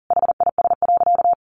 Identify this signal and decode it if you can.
Morse code, heard as a keyed tone, about 40 wpm, 700 Hz. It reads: HIS.
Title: Sentences from Top 1500 Words 40wpm